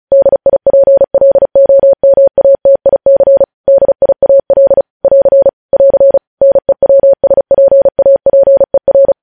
Morse code, heard as a keyed tone, about 36 wpm, 550 Hz. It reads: DIPLOMATIC DIAL <AR> <AR> NEWSPAPER
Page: News Headlines in Morse Code at 35 WPM